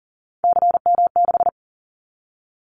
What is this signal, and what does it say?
Morse code, heard as a keyed tone, about 42 wpm, 700 Hz. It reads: CM6
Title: Callsign Prefixes for Top 100 DXCC - 40wpm